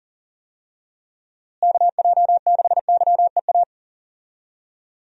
Morse code, heard as a keyed tone, about 40 wpm, 700 Hz. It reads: KJ6YEA
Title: USA Call Signs - 6 Characters - Repeated 2x - 40wpm